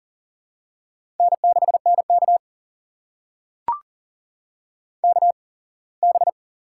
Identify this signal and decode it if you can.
Morse code, heard as a keyed tone, about 40 wpm, spaced as for 24 wpm, 700 Hz. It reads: N6NK KB
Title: USA Call Signs - 8x Character Spacing - 40wpm